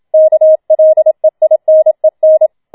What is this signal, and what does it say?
Morse code, heard as a keyed tone, about 27 wpm, 630 Hz. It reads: KLEINEN
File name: Schneewittchen Teil2 27wpm